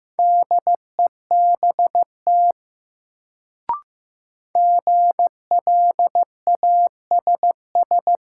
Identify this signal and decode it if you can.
Morse code, heard as a keyed tone, about 15 wpm, 700 Hz. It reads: DEBT GLASS
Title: Sentences from Top 1300 Words - New Word Drill Alt - 15wpm